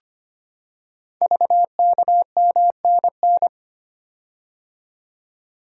Morse code, heard as a keyed tone, about 25 wpm, 700 Hz. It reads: VKMNN
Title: Code Groups - 5 Letters - 25wpm